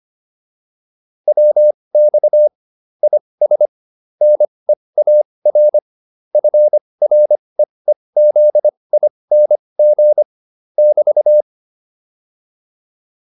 Morse code, heard as a keyed tone, about 25 wpm, spaced as for 20 wpm, 600 Hz. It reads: WX IS NEAR FREEZING <BT>